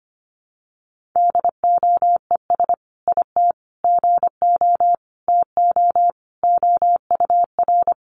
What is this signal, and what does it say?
Morse code, heard as a keyed tone, about 25 wpm, 700 Hz. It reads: DOES IT GO TO OUR
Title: Sentences from Top 1500 Words 25wpm